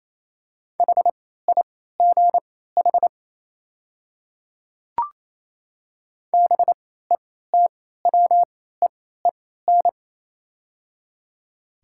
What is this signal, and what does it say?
Morse code, heard as a keyed tone, about 27 wpm, spaced as for 16 wpm, 700 Hz. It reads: HIGH BETWEEN
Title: 3x Character Spacing - Top 300 Words - 28wpm